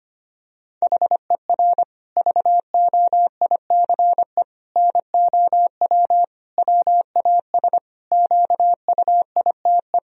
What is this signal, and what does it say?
Morse code, heard as a keyed tone, about 25 wpm, 700 Hz. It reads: HER VOICE NOW WAS QUITE
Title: Sentences from Top 1000 Words - Review Part 3 - 25wpm